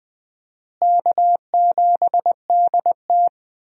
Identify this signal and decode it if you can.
Morse code, heard as a keyed tone, about 20 wpm, 700 Hz. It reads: K7DT
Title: USA Call Signs - 4 Characters - Repeated 2x - 20wpm